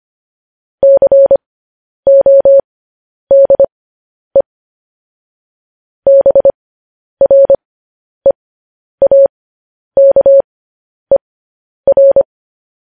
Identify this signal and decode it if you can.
Morse code, heard as a keyed tone, about 25 wpm, spaced as for 10 wpm, 550 Hz. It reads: CODE BREAKER
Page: News Headlines in Morse Code at 10 WPM